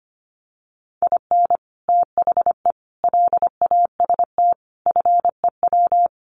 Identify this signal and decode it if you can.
Morse code, heard as a keyed tone, about 25 wpm, 700 Hz. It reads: IN THE LAST FEW
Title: Sets of 5 Words Encore2 25wpm